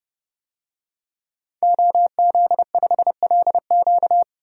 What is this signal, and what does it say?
Morse code, heard as a keyed tone, about 30 wpm, 700 Hz. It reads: OZ5LQ
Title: DXCC-100 Call Signs - 5 Characters - 30wpm